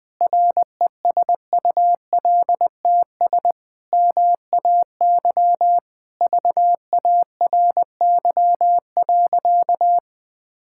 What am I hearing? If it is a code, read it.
Morse code, heard as a keyed tone, about 21 wpm, 700 Hz. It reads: RESULTS MAY VARY.
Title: Sentences with 6-Word Phrases - 20wpm